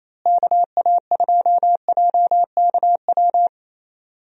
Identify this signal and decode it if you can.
Morse code, heard as a keyed tone, about 28 wpm, 700 Hz. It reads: KA2JKW
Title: USA Call Signs - 6 Characters - Repeated 3x - 28wpm